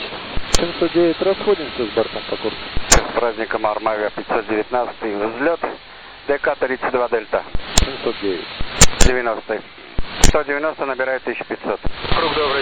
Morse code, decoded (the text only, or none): none